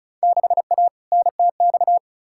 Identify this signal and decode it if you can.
Morse code, heard as a keyed tone, about 34 wpm, 700 Hz. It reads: 6A NTX
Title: Field Day Exchanges - 35wpm